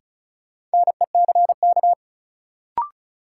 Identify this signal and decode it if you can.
Morse code, heard as a keyed tone, about 36 wpm, 700 Hz. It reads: NECK
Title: Sets of 5 Words Encore - New Word Drill - 35wpm